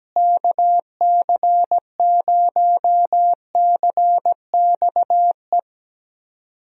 Morse code, heard as a keyed tone, about 17 wpm, 700 Hz. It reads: KC0CXE
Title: USA Call Signs - 6 Characters - Repeated 2x - 17wpm